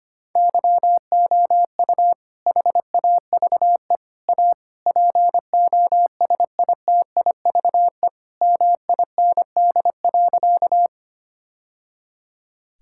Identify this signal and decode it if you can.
Morse code, heard as a keyed tone, about 25 wpm, 700 Hz. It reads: YOU HAVE A POSITIVE MIND.
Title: Sentences from Top 1000 Words - Review Part 5 - 25wpm